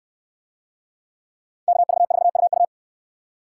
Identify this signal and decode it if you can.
Morse code, heard as a keyed tone, about 64 wpm, 700 Hz. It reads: KV4UU